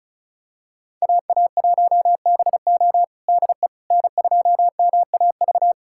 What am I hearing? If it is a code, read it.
Morse code, heard as a keyed tone, about 36 wpm, 700 Hz. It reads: AA1BO DE N2MAV
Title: USA Callsign DE Callsign - 5 Characters - 35wpm